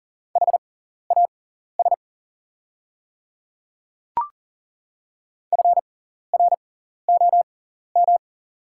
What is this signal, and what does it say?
Morse code, heard as a keyed tone, about 40 wpm, spaced as for 14 wpm, 700 Hz. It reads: HAS FROM
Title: 6x Character Spacing - Top 200 Words - 40wpm